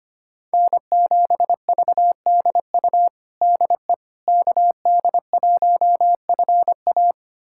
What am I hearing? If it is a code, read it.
Morse code, heard as a keyed tone, about 25 wpm, 700 Hz. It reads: N7VDU DE KD1FA
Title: USA Callsign DE Callsign - 5 Characters - 25wpm